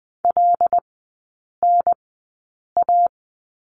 Morse code, heard as a keyed tone, about 20 wpm, spaced as for 8 wpm, 700 Hz. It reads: LNA
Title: 218 Lesson 35e - Number 0 - Three Characters 20wpm